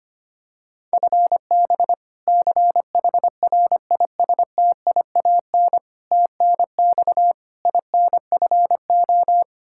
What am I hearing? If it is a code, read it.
Morse code, heard as a keyed tone, about 25 wpm, 700 Hz. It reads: FB CHRISTIAN TNX INFO
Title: K1USN SST QSO Practice 25wpm